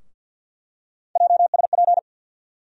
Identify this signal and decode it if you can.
Morse code, heard as a keyed tone, about 48 wpm, 700 Hz. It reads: JSP
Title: USA Call Sign Postfixes - 3 Letters 50wpm